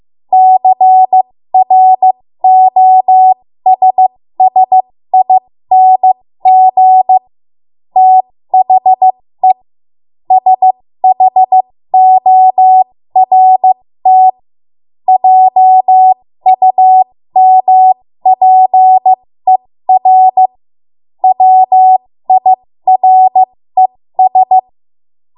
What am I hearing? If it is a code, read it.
Morse code, heard as a keyed tone, about 15 wpm, spaced as for 13 wpm, 750 Hz. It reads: CROSSING THE SHORT JUMPER WIRES
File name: Morse / 13WPM